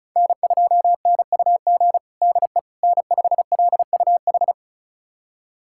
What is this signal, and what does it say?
Morse code, heard as a keyed tone, about 34 wpm, 700 Hz. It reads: N2NUG DE N5LUH
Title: USA Callsign DE Callsign - 5 Characters - 35wpm